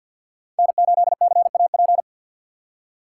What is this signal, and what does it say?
Morse code, heard as a keyed tone, about 47 wpm, 690 Hz. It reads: N8CAP
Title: USA Call Signs - 5 Characters - 2x Character Spacing - 50wpm